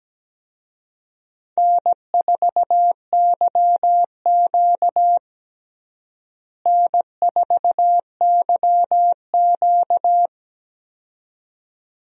Morse code, heard as a keyed tone, about 17 wpm, 700 Hz. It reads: N4YQ N4YQ
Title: USA Call Signs - 4 Characters - Repeated 2x - 17wpm